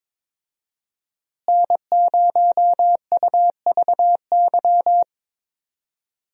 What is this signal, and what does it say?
Morse code, heard as a keyed tone, about 22 wpm, 700 Hz. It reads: N0UVY
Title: USA Call Signs - 5 Characters - Repeated 2x - 22wpm